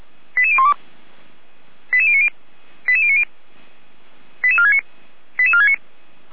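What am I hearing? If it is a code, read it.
Five-tone selective call (ZVEI), heard as quick runs of five tones, tones of about 70 ms. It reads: 88012 88090 88090 88469 88469
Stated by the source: ZVEI five-tone selective call code 88192, received by SDR, from fire brigade Feuerwehr Orlamünde's audio server.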